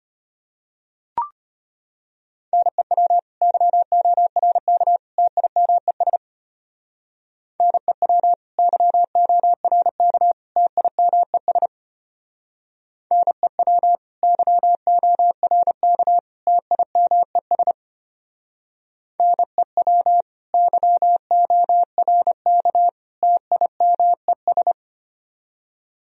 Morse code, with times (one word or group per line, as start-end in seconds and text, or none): 2.5-3.2 NEW
3.4-5.0 YORK
5.2-6.2 TIMES
7.6-8.3 NEW
8.6-10.3 YORK
10.6-11.7 TIMES
13.1-14.0 NEW
14.2-16.2 YORK
16.5-17.7 TIMES
19.2-20.2 NEW
20.5-22.9 YORK
23.2-24.7 TIMES